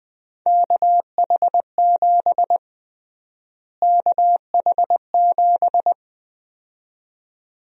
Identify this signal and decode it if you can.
Morse code, heard as a keyed tone, about 20 wpm, 700 Hz. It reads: KH7 KH7
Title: USA Call Sign Prefixes - Repeated 2x - 20wpm